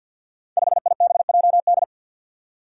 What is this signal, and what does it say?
Morse code, heard as a keyed tone, about 50 wpm, 700 Hz. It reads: 5IBJD
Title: Five Challenging Letter-Number Combinations 50wpm